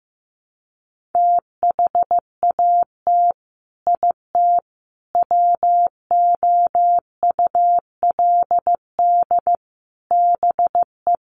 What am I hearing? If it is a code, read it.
Morse code, heard as a keyed tone, about 15 wpm, 700 Hz. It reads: THAT IT WOULD BE
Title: Sets of 4 Words 15wpm